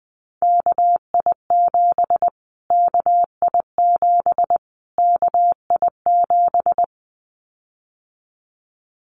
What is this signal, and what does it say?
Morse code, heard as a keyed tone, about 20 wpm, 700 Hz. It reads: KI7 KI7 KI7
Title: Callsign Prefixes for Top 25 DXCC - Repeated 3x - 20wpm